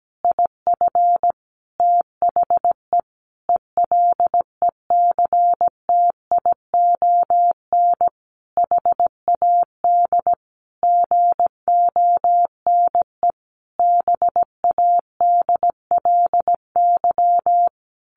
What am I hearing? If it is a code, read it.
Morse code, heard as a keyed tone, about 17 wpm, 700 Hz. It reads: IF THE ELECTION HAD GONE BADLY
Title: A Sound Of Thunder 17wpm